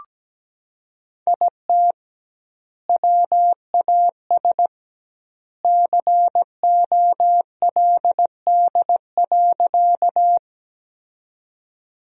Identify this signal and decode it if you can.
Morse code, heard as a keyed tone, about 17 wpm, 700 Hz. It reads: IT WAS COLD.
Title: 2x Word-Spacing - Sentences from Top 400 Words - 17wpm